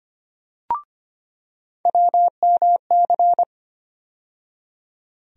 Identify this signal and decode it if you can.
Morse code, heard as a keyed tone, about 25 wpm, 700 Hz. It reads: WMC
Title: USA Call Sign Postfixes - 3 Letters 25wpm